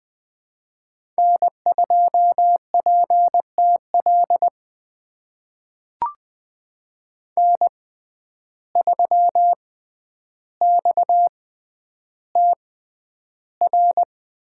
Morse code, heard as a keyed tone, about 20 wpm, spaced as for 10 wpm, 700 Hz. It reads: N2PTL N3XTR
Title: USA Call Signs - 5 Characters - 6x Character Spacing - 20wpm